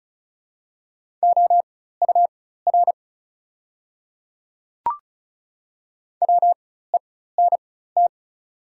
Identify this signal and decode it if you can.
Morse code, heard as a keyed tone, about 33 wpm, spaced as for 16 wpm, 690 Hz. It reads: OUR WENT
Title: 4x Character Spacing - Top 200 Words - 35wpm